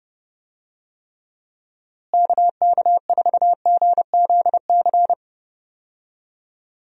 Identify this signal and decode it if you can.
Morse code, heard as a keyed tone, about 30 wpm, 700 Hz. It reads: KK4GZC
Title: USA Call Signs - 6 Characters - Repeated 2x - 30wpm